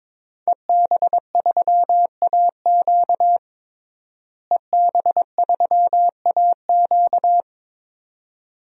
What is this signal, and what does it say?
Morse code, heard as a keyed tone, about 22 wpm, 700 Hz. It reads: EB3AQ EB3AQ
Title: DXCC-100 Call Signs - 5 Characters - Repeated 2x - 22wpm